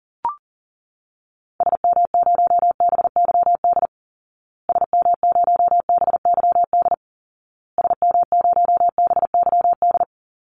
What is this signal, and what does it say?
Morse code, heard as a keyed tone, about 40 wpm, 700 Hz. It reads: SM0BYD SM0BYD SM0BYD
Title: Call Signs Repeated 2x - 40wpm